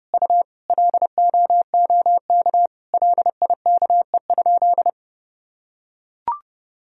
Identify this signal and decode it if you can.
Morse code, heard as a keyed tone, about 31 wpm, 700 Hz. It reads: U LOOK LIKE?
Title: Sentences from Top 100 Words and Abbreviations 30wpm